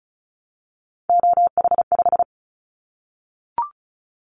Morse code, heard as a keyed tone, about 34 wpm, 700 Hz. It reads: OH5
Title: Callsign Prefixes for Top 25 DXCC - Repeated 3x - 35wpm